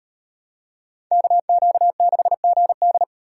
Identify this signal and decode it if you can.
Morse code, heard as a keyed tone, about 36 wpm, 700 Hz. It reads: KQ6GD